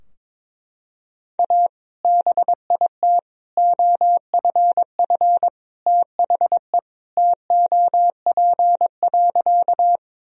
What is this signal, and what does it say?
Morse code, heard as a keyed tone, about 21 wpm, 700 Hz. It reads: A BIT OFF THE TOP.
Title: Sentences from Top 1000 Words - Review Part 2 - 22wpm